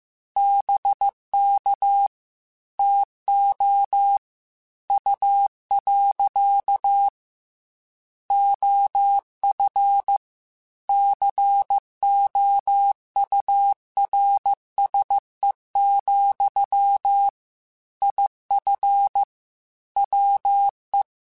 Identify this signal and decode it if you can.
Morse code, heard as a keyed tone, about 15 wpm, 800 Hz. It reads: BK TO U. OF COURSE, IF WE
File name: QSOs / QSOs-mp3s / WA7QOX 1@13